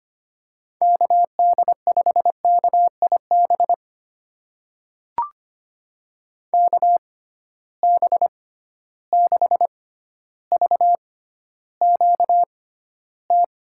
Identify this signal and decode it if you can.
Morse code, heard as a keyed tone, about 25 wpm, spaced as for 13 wpm, 700 Hz. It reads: KD5KIB KB6VQT
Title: USA Call Signs - 6 Characters - 6x Character Spacing - 25wpm